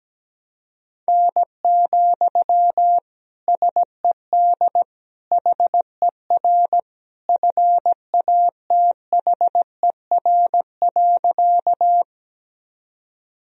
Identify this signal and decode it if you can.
Morse code, heard as a keyed tone, about 17 wpm, 700 Hz. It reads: N, SED HER FATHER.
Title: Sentences from Top 300 Words and Abbreviations 17wpm